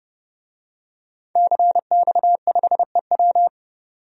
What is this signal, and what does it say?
Morse code, heard as a keyed tone, about 30 wpm, 700 Hz. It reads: CX5EW
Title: DXCC-100 Call Signs - 5 Characters - 30wpm